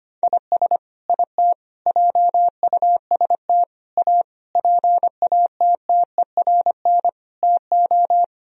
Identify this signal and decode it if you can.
Morse code, heard as a keyed tone, about 25 wpm, 700 Hz. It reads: IS IT JUST A PATTERN TO
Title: Sentences from Top 1000 Words - Review Part 2 - 25wpm